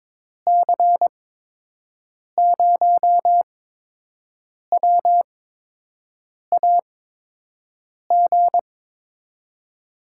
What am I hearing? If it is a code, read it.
Morse code, heard as a keyed tone, about 22 wpm, 700 Hz. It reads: C 0 W A G
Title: USA Call Signs - 6 Characters - 8x Character Spacing - 22wpm